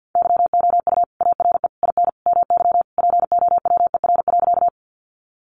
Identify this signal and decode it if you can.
Morse code, heard as a keyed tone, about 50 wpm, 700 Hz. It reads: YOU ARE IN MY POWER.
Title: Sentences from Top 500 Words 50wpm